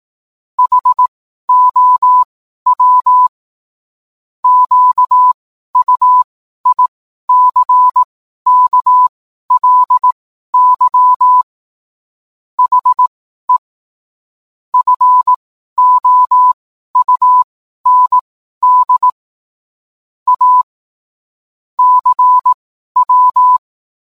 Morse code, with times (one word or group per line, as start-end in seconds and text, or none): 0.6-3.3 HOW
4.4-11.4 QUICKLY
12.6-13.6 HE
14.7-19.1 FOUND
20.3-20.6 A
21.8-23.6 CW